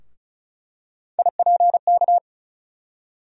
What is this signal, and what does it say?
Morse code, heard as a keyed tone, about 36 wpm, 700 Hz. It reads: IPK
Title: USA Call Sign Postfixes - 3 Letters 35wpm